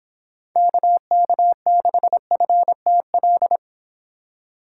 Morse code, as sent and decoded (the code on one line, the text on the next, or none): -.- -.- -.... ..-. - .-..
KK6FTL